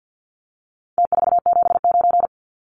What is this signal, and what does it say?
Morse code, heard as a keyed tone, about 50 wpm, 700 Hz. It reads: T479
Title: Numbers - Contest Style - 4-Digit 50wpm